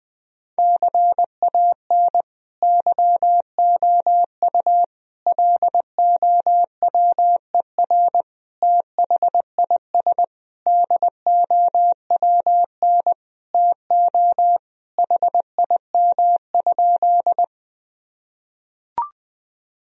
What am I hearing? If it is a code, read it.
Morse code, heard as a keyed tone, about 20 wpm, 700 Hz. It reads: CAN YOU LOWER THIS DOWN TO HIM?
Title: Sentences from Top 1000 Words - Review Part 4 - 20wpm